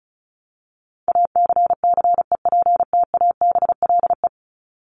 Morse code, heard as a keyed tone, about 34 wpm, 700 Hz. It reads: ACCEPTABLE
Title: Top 1501-1750 Words 35wpm